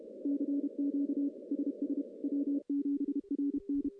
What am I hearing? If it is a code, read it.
Morse code, heard as a keyed tone, about 32 wpm, 280 Hz. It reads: CQ SS W7RN
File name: contests / CW-SS-k5tr- -1500